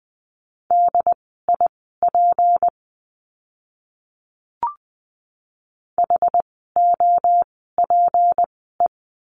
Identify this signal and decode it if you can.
Morse code, heard as a keyed tone, about 20 wpm, spaced as for 14 wpm, 700 Hz. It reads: DIP HOPE
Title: 153 Lesson 25f - Letter p - Words 20wpm